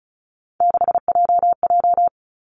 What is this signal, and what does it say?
Morse code, heard as a keyed tone, about 36 wpm, 700 Hz. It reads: 6JJ
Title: Three Challenging Letter-Number Combinations 35wpm